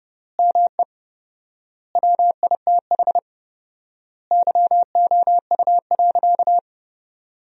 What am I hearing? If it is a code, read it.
Morse code, heard as a keyed tone, about 30 wpm, 700 Hz. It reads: ME WITH YOU.
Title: 4x Word-Spacing - Sentences from Top 200 Words - 30wpm